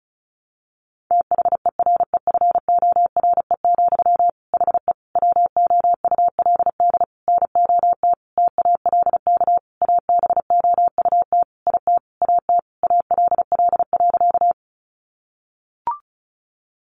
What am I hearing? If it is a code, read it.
Morse code, heard as a keyed tone, about 34 wpm, 700 Hz. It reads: THEREFORE, HE WOULD NOT TALK ABOUT IT AT ALL.
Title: Sentences from Top 800 Words 35wpm